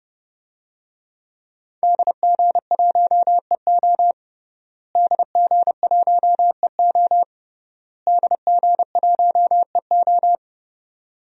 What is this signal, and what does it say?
Morse code, heard as a keyed tone, about 31 wpm, 700 Hz. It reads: DG1EO DG1EO DG1EO
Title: DXCC-100 Call Signs - 5 Characters - Repeated 3x - 30wpm